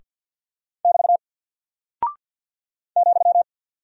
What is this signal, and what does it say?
Morse code, heard as a keyed tone, about 48 wpm, 700 Hz. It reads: <BT> ,
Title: Single Character (Letter:Number:Punctuation:Prosign) Alt 50wpm